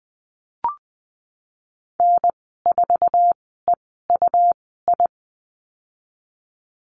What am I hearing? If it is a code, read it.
Morse code, heard as a keyed tone, about 20 wpm, spaced as for 15 wpm, 700 Hz. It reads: N4EUI
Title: 128 Lesson 21g - Number 6 - Call Signs 20wpm